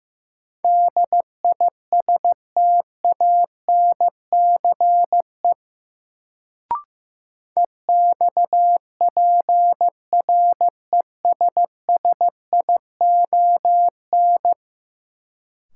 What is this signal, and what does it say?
Morse code, heard as a keyed tone, about 16 wpm, 700 Hz. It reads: DISTANCE EXPRESSION
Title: Sentences from Top 1200 Words - New Word Drill - 15wpm